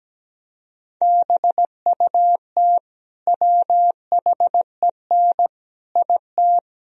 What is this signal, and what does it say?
Morse code, heard as a keyed tone, about 17 wpm, 700 Hz. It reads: BUT WHEN IT
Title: Speed-Racing - Sets of 5 Words 17wpm